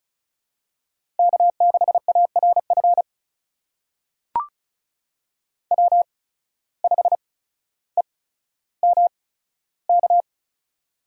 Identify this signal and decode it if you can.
Morse code, heard as a keyed tone, about 36 wpm, spaced as for 15 wpm, 700 Hz. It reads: K6ARF W5EMK
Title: USA Call Signs - 5 Characters - 8x Character Spacing - 35wpm